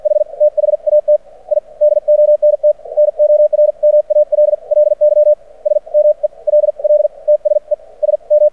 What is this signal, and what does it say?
Morse code, heard as a keyed tone, about 53 wpm, spaced as for 42 wpm, 600 Hz. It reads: H THAT I DONT NORMALLY SMELL TS E S M
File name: archived / CRSnet